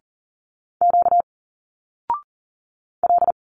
Morse code, heard as a keyed tone, about 40 wpm, 700 Hz. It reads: Q L
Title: Single Letters 40wpm